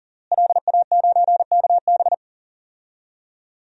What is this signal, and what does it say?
Morse code, heard as a keyed tone, about 40 wpm, 700 Hz. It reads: LA9KB
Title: DXCC-100 Call Signs - 5 Characters - Repeated 3x - 40wpm